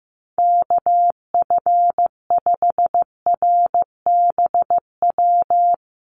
Morse code, heard as a keyed tone, about 15 wpm, 700 Hz. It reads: KF5RBW
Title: USA Call Signs Repeated 3x 15wpm